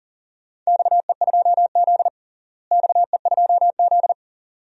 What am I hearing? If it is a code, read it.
Morse code, heard as a keyed tone, about 40 wpm, 700 Hz. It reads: XE2Z XE2Z
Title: DXCC-100 Call Signs - 4 Characters - Repeated 2x - 40wpm